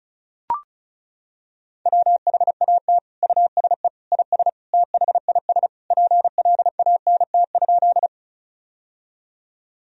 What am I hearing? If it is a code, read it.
Morse code, heard as a keyed tone, about 36 wpm, 700 Hz. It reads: WHAT USE IS THIS PLANT?
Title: Sentences from Top 1000 Words - Review Part 1 - 35wpm